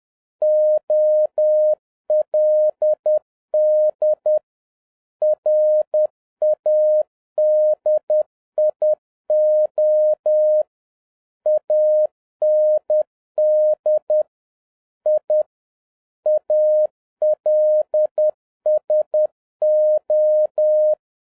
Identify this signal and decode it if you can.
Morse code, heard as a keyed tone, about 10 wpm, 610 Hz. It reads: OLD RADIO AND I ALSO